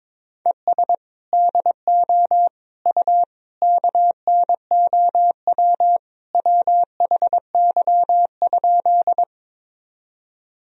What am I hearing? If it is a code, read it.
Morse code, heard as a keyed tone, about 22 wpm, 700 Hz. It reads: ES DO U KNOW WHY?
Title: Sentences from Top 200 Words and Abbreviations 22wpm